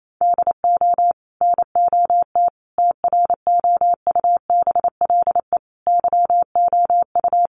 Morse code, heard as a keyed tone, about 28 wpm, 700 Hz. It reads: DO NOT TROUBLE YOU
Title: Sentences from Top 1200 Words 28wpm